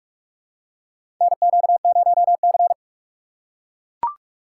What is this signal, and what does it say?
Morse code, heard as a keyed tone, about 42 wpm, 700 Hz. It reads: NQ0C
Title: USA Call Signs - 4 Characters - 4x Character Spacing - 45wpm